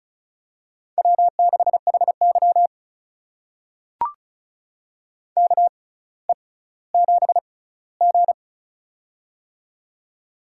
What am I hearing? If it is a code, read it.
Morse code, heard as a keyed tone, about 36 wpm, spaced as for 18 wpm, 700 Hz. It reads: W6HY KE7G
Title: USA Call Signs - 4 Characters - 6x Character Spacing - 35wpm